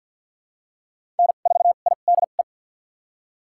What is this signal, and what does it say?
Morse code, heard as a keyed tone, about 53 wpm, spaced as for 37 wpm, 700 Hz. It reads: N4IDE
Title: USA Call Signs - 5 Characters - 2x Character Spacing - 50wpm